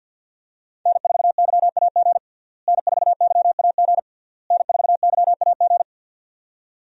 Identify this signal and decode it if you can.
Morse code, heard as a keyed tone, about 51 wpm, 700 Hz. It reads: N4YAG N4YAG N4YAG
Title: USA Call Signs - 5 Characters - Repeated 3x - 50wpm